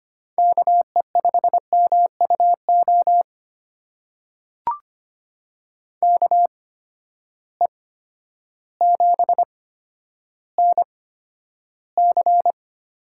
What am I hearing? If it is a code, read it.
Morse code, heard as a keyed tone, about 25 wpm, spaced as for 11 wpm, 700 Hz. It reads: KE5MUO KE7NC
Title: USA Call Signs - 6 Characters - 8x Character Spacing - 25wpm